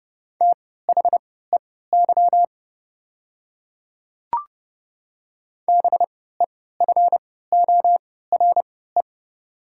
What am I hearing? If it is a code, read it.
Morse code, heard as a keyed tone, about 32 wpm, spaced as for 17 wpm, 690 Hz. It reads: THEY BEFORE
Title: 3x Character Spacing - Top 200 Words - 30wpm